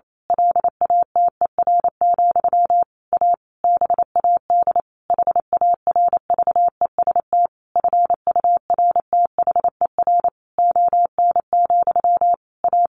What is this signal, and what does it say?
Morse code, heard as a keyed tone, about 28 wpm, 700 Hz. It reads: LATER, A BAD HARVEST FURTHER ON, A
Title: A Sound Of Thunder 28wpm